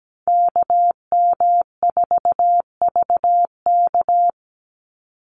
Morse code, heard as a keyed tone, about 17 wpm, 700 Hz. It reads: KM4VK
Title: USA Call Signs 17wpm